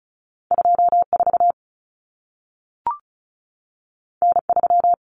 Morse code, heard as a keyed tone, about 36 wpm, 700 Hz. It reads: 24 N3
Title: Common Cut Numbers - 2-Digit 35wpm